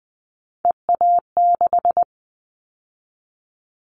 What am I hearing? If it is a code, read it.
Morse code, heard as a keyed tone, about 20 wpm, 700 Hz. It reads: EA6
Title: Callsign Prefixes for Top 50 DXCC - 20wpm